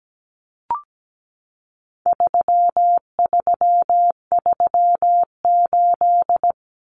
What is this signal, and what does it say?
Morse code, heard as a keyed tone, about 17 wpm, 700 Hz. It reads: 3338
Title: Common Cut Numbers - 4-Digit 17wpm